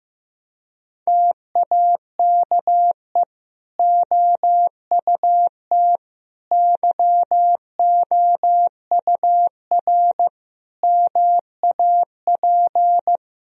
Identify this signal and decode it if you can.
Morse code, heard as a keyed tone, about 16 wpm, 700 Hz. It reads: TAKE OUT YOUR MAP
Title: Sentences from Top 1000 Words - Review Part 2 - 15wpm